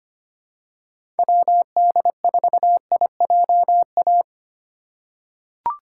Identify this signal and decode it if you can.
Morse code, heard as a keyed tone, about 25 wpm, 700 Hz. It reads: WD4IJA
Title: USA Call Signs - 6 Characters - 3x Character Spacing - 25wpm